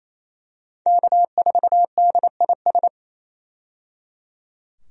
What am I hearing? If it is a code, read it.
Morse code, heard as a keyed tone, about 28 wpm, 700 Hz. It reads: K4DIS
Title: USA Call Signs - 5 Characters - 28wpm